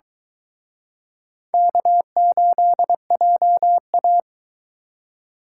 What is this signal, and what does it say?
Morse code, heard as a keyed tone, about 22 wpm, 700 Hz. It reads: K8JA